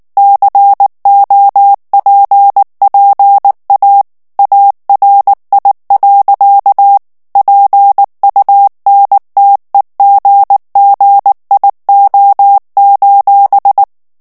Morse code, heard as a keyed tone, about 19 wpm, 800 Hz. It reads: COPPA ARI. PUNTEGGIO:
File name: morse mp3 / ITALIEN texte clair 2 vitesse 18 mots minute